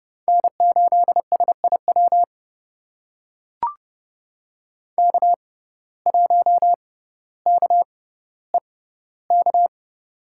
Morse code, heard as a keyed tone, about 30 wpm, spaced as for 15 wpm, 700 Hz. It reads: N8SIW K1KEK
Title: USA Call Signs - 5 Characters - 6x Character Spacing - 30wpm